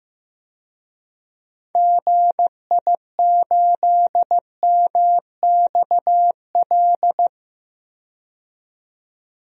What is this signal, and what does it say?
Morse code, heard as a keyed tone, about 16 wpm, 700 Hz. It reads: GI8MXL
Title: DXCC-100 Call Signs - 6 Characters - 15wpm